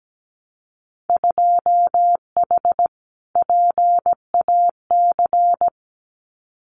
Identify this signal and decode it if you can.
Morse code, heard as a keyed tone, about 17 wpm, 700 Hz. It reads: 2H PAC
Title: Winter Field Day Exchanges Encore 17wpm